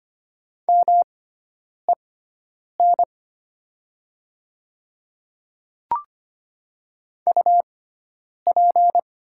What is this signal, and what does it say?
Morse code, heard as a keyed tone, about 25 wpm, spaced as for 9 wpm, 700 Hz. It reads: MEN UP
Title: 6x Character Spacing - Top 300 Words - 25wpm